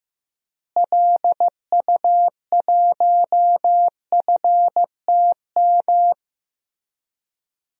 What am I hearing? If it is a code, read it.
Morse code, heard as a keyed tone, about 16 wpm, 700 Hz. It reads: LU1FTM
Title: DXCC-100 Call Signs - 6 Characters - Repeated 3x - 15wpm